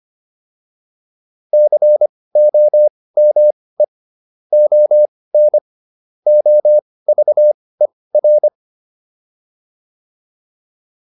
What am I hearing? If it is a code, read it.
Morse code, heard as a keyed tone, about 25 wpm, spaced as for 18 wpm, 600 Hz. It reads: COME ON OVER
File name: PR305 18